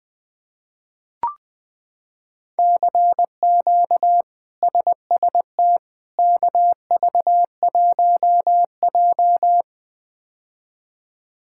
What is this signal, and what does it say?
Morse code, heard as a keyed tone, about 20 wpm, 700 Hz. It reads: CQ SST KV1J
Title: K1USN SST QSO Practice - 20wpm